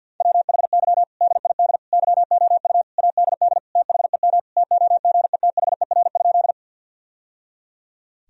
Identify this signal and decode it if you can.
Morse code, heard as a keyed tone, about 49 wpm, 700 Hz. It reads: WHY DID YOU ADD THEM TOGETHER?
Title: Sentences from Top 1000 Words - Review Part 1 - 50wpm